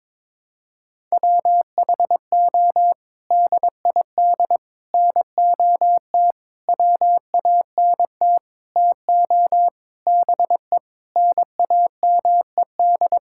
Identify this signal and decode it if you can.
Morse code, heard as a keyed tone, about 23 wpm, 700 Hz. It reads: WHO DID NOT WANT TO BE NAMED
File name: practice / 22wpm / Sets of 7 Words - 22wpm